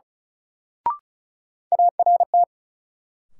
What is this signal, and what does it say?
Morse code, heard as a keyed tone, about 36 wpm, 700 Hz. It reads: ART
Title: Sentences from Top 600 Words - New Word Drill - 35wpm